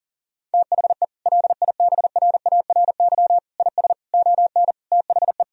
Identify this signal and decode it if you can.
Morse code, heard as a keyed tone, about 40 wpm, 700 Hz. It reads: THE LIBRARY IS ON THE
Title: Sentences from Top 1000 Words - Review Part 4 - 40wpm